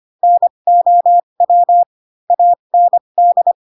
Morse code, heard as a keyed tone, about 25 wpm, spaced as for 22 wpm, 700 Hz. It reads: NOW AND
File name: Pinocchio / Pinocchio ch6 18 25 ik1hge